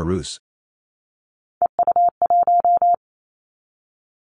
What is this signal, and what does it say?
Morse code, heard as a keyed tone, about 28 wpm, 700 Hz. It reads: EU1
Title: Callsign Prefixes for Top 50 DXCC - 28wpm